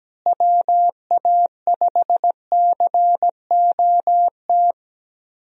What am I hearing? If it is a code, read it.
Morse code, heard as a keyed tone, about 17 wpm, 700 Hz. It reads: WA5COT
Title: USA Call Signs - 6 Characters - Repeated 3x - 17wpm